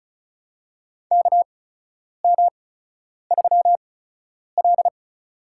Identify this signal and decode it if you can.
Morse code, heard as a keyed tone, about 34 wpm, spaced as for 10 wpm, 700 Hz. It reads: KM3L
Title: USA Call Signs - 4 Characters - 8x Character Spacing - 35wpm